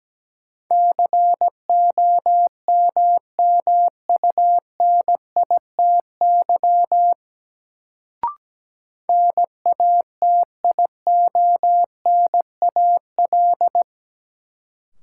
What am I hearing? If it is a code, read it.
Morse code, heard as a keyed tone, about 17 wpm, 700 Hz. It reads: COMMUNITY NATIONAL
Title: Sentences from Top 1200 Words - New Word Drill Alt - 17wpm